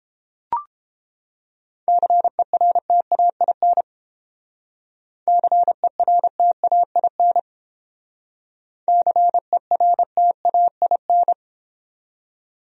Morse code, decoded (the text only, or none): CERTAIN CERTAIN CERTAIN